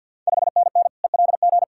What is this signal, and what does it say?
Morse code, heard as a keyed tone, about 53 wpm, 700 Hz. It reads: 5NN ELG
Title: Ontario QSO Party Exchange 50wpm